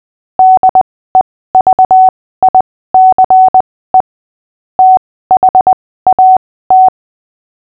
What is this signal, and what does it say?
Morse code, heard as a keyed tone, about 20 wpm, spaced as for 15 wpm, 750 Hz. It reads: DEVICE THAT
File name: cw / fists / Short-Story-6-15wpm